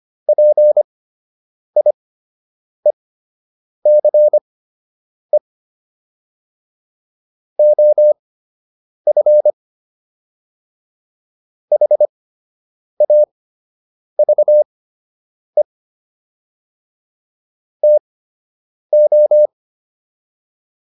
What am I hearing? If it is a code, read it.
Morse code, heard as a keyed tone, about 25 wpm, spaced as for 8 wpm, 600 Hz. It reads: PIECE OF HAVE TO